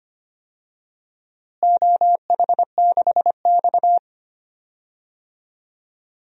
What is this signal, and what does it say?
Morse code, heard as a keyed tone, about 25 wpm, 700 Hz. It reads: OH6X
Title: DXCC-100 Call Signs - 4 Characters - Repeated 3x - 25wpm